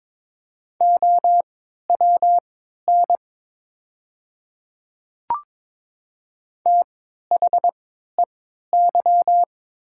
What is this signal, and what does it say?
Morse code, heard as a keyed tone, about 22 wpm, spaced as for 13 wpm, 690 Hz. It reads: OWN THEY
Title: 3x Character Spacing - Top 300 Words - 22wpm